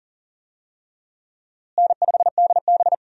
Morse code, heard as a keyed tone, about 37 wpm, 700 Hz. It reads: N5DB
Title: USA Call Signs - 4 Characters - Repeated 3x - 40wpm